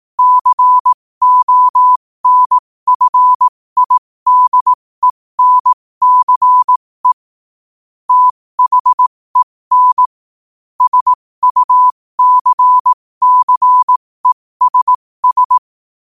Morse code, heard as a keyed tone, about 18 wpm, spaced as for 15 wpm, 1000 Hz. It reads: CONFIDENCE THEN SUCCESS
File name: qotdfiles / 15wpm